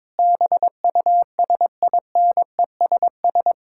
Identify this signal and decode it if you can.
Morse code, heard as a keyed tone, about 22 wpm, 700 Hz. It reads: BUSINESS
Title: Sentences from Top 1000 Words - Review Part 5 - 22wpm